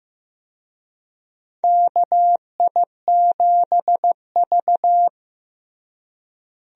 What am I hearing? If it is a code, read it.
Morse code, heard as a keyed tone, about 15 wpm, 700 Hz. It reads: KI7V
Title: USA Call Signs - 4 Characters - Repeated 2x - 15wpm